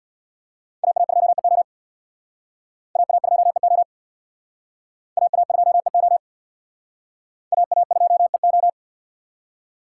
Morse code, heard as a keyed tone, about 60 wpm, 700 Hz. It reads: AA2EO AA2EO AA2EO AA2EO
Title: Speed Racing - USA Call Signs 50wpm